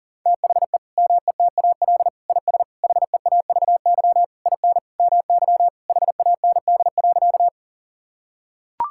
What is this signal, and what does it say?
Morse code, heard as a keyed tone, about 40 wpm, 700 Hz. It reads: THE METAL IS HEAVY IN MY HAND.
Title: Sentences from Top 1000 Words - Review Part 6 - 40wpm